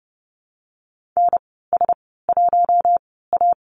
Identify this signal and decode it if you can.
Morse code, heard as a keyed tone, about 30 wpm, spaced as for 17 wpm, 700 Hz. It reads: NS1A
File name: Lesson 14g - Number 2 - Call Signs 30wpm